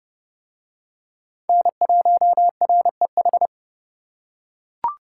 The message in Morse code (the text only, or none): N1REH